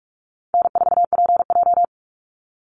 Morse code, heard as a keyed tone, about 44 wpm, 700 Hz. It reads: N4PJ
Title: Call Signs - 45wpm